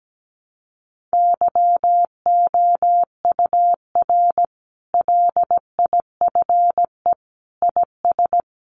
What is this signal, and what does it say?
Morse code, heard as a keyed tone, about 17 wpm, 700 Hz. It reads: YOUR LIFE IS
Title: Sentences from Top 1200 Words 17wpm